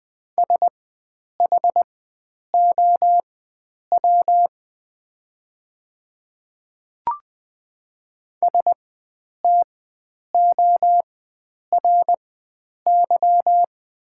Morse code, heard as a keyed tone, about 20 wpm, spaced as for 9 wpm, 700 Hz. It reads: SHOW STORY
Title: 4x Character Spacing - Top 500 Words - 20wpm